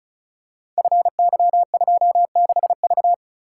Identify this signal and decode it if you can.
Morse code, heard as a keyed tone, about 36 wpm, 700 Hz. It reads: FY26V
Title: Code Groups - 5 Letter:Numbers - 35wpm